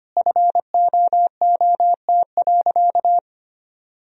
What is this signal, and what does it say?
Morse code, heard as a keyed tone, about 25 wpm, 700 Hz. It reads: FOOT.
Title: Sentences from Top 1000 Words - Review Part 3 - 25wpm